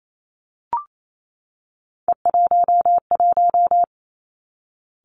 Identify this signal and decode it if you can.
Morse code, heard as a keyed tone, about 28 wpm, 700 Hz. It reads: E11
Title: Three Challenging Letter-Number Combinations 28wpm